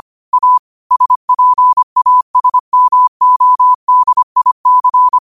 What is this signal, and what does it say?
Morse code, heard as a keyed tone, about 25 wpm, 1000 Hz. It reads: A SPASMODIC
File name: qotdfiles / 25wpm